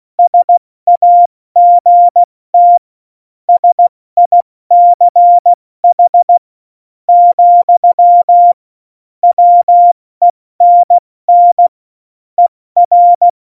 Morse code, heard as a keyed tone, about 16 wpm, spaced as for 13 wpm, 700 Hz. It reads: SAGT SICH, WENN ER
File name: Morsekurs / Witze / Witz 16. April 2019, 70 F80